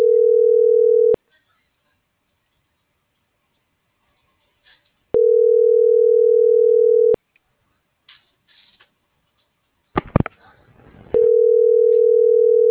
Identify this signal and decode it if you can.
Ringback tone (North America), 440 + 480 Hz.